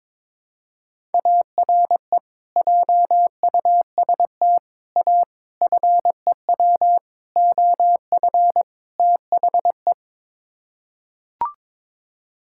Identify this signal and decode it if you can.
Morse code, heard as a keyed tone, about 22 wpm, 700 Hz. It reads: ARE JUST A FEW OF THE
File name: practice / 22wpm / Sets of 6 Words - 22wpm